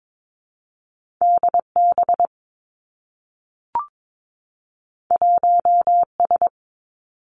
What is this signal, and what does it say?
Morse code, heard as a keyed tone, about 22 wpm, 700 Hz. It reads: D B 1 S
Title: Two Challenging Letter-Number Combinations 22wpm